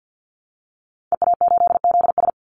Morse code, heard as a keyed tone, about 49 wpm, 700 Hz. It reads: EA8ZS
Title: Call Signs - 50wpm